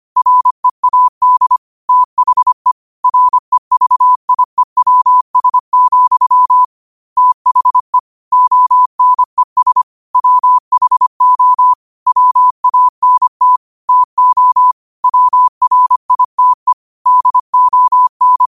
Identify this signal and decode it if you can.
Morse code, heard as a keyed tone, about 25 wpm, 1000 Hz. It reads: READ THE REVIEWS, THE ONES WHO WANT TO WRITE DON
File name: qotdfiles / 25wpm